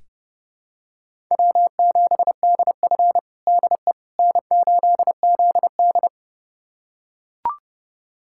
Morse code, heard as a keyed tone, about 30 wpm, 700 Hz. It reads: W7DF DE N8ZD
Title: USA Callsign DE Callsign - 4 Characters - 30wpm